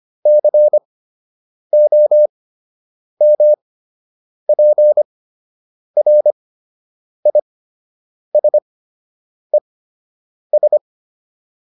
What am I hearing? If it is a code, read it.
Morse code, heard as a keyed tone, about 25 wpm, spaced as for 8 wpm, 600 Hz. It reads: COMPRISES